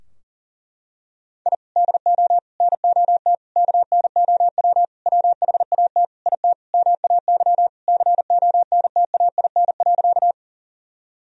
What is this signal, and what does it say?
Morse code, heard as a keyed tone, about 40 wpm, 700 Hz. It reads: I DO NOT KNOW WHAT IT MAY CONTAIN.
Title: Sentences from Top 1000 Words - Review Part 2 - 40wpm